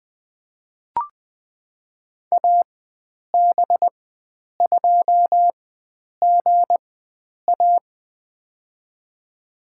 Morse code, heard as a keyed tone, about 20 wpm, spaced as for 9 wpm, 700 Hz. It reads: AB2GA
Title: USA Call Signs - 5 Characters - 4x Character Spacing - 20wpm